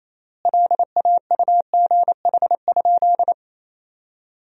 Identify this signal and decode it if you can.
Morse code, heard as a keyed tone, about 28 wpm, 700 Hz. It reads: LAUGH?
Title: 3x Word-Spacing - Sentences from Top 500 Words - 28wpm